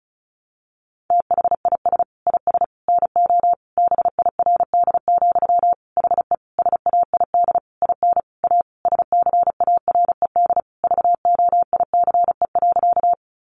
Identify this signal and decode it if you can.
Morse code, heard as a keyed tone, about 34 wpm, 700 Hz. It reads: THIS IS NO BIRD, HE SAID IN A SCARED VOICE.
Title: Sentences from Top 700 Words 35wpm